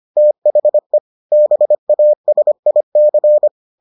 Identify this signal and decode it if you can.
Morse code, heard as a keyed tone, about 25 wpm, 600 Hz. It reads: THE BASIC